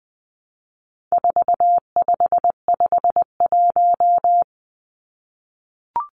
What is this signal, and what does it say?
Morse code, heard as a keyed tone, about 20 wpm, 700 Hz. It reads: 4551
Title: Numbers - 4-Digit 20wpm